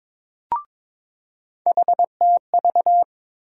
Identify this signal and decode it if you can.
Morse code, heard as a keyed tone, about 22 wpm, 700 Hz. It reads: HTV
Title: USA Call Sign Postfixes - 3 Letters 22wpm